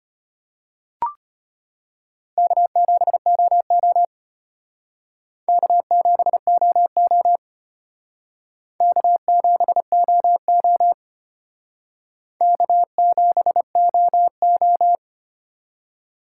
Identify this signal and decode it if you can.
Morse code, heard as a keyed tone, about 33 wpm, 700 Hz. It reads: K7OO K7OO K7OO K7OO